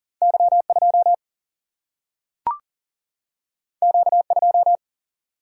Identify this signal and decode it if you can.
Morse code, heard as a keyed tone, about 40 wpm, 700 Hz. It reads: Y2 Q2
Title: Single-Letter Single-Number 40wpm